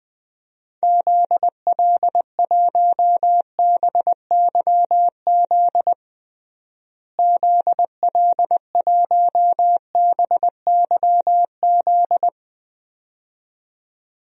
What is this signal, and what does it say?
Morse code, heard as a keyed tone, about 20 wpm, 700 Hz. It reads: ZL1BYZ ZL1BYZ
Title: DXCC-100 Call Signs - 6 Characters - Repeated 2x - 20wpm